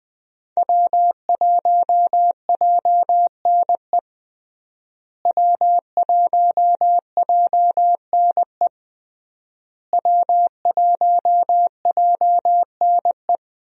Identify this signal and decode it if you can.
Morse code, heard as a keyed tone, about 21 wpm, 700 Hz. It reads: W1JNE W1JNE W1JNE
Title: USA Call Signs - 5 Characters - Repeated 3x - 20wpm